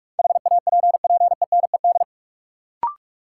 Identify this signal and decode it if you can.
Morse code, heard as a keyed tone, about 46 wpm, 700 Hz. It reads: HAPPENED
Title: Sentences from Top 1000 Words - Review Part 2 - New Word Drill - 45wpm